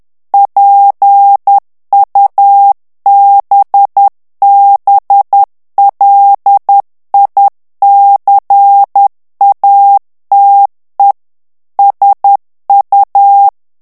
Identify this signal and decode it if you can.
Morse code, heard as a keyed tone, about 10 wpm, 800 Hz. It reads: PUBBLICATE SU
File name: ITALIEN texte clair 2 vitesse 10 mots minute